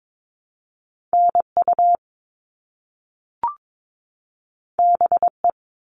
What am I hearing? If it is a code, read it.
Morse code, heard as a keyed tone, about 22 wpm, 700 Hz. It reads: NU BE
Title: All Cut Numbers - 2-Digit 22wpm